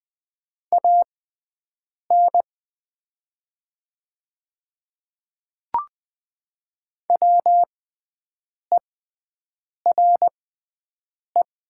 Morse code, heard as a keyed tone, about 20 wpm, spaced as for 7 wpm, 690 Hz. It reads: AN WERE
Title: 6x Character Spacing - Top 100 Words - 20wpm